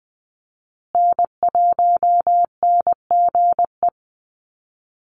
Morse code, heard as a keyed tone, about 20 wpm, 700 Hz. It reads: N1NGE
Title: USA Call Signs Repeated 3x 20wpm